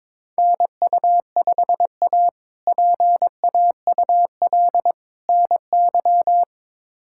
Morse code, heard as a keyed tone, about 22 wpm, 700 Hz. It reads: NU5A PAUL NY
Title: K1USN SST QSO Practice - 22wpm